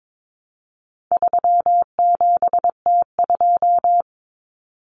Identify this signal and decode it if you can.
Morse code, heard as a keyed tone, about 22 wpm, 700 Hz. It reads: 37T2
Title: Common Cut Numbers - 4-Digit 22wpm